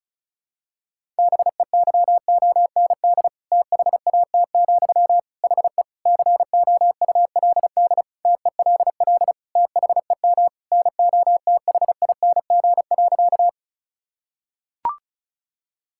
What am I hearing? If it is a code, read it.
Morse code, heard as a keyed tone, about 36 wpm, 700 Hz. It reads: BEYOND THAT, HE COULD TELL THEM NOTHING.
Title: Sentences from Top 1000 Words - Review Part 6 - 35wpm